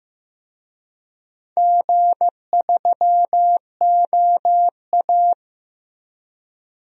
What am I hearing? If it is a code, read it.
Morse code, heard as a keyed tone, about 15 wpm, 700 Hz. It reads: G3OA
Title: DXCC-100 Call Signs - 4 Characters - Repeated 2x - 15wpm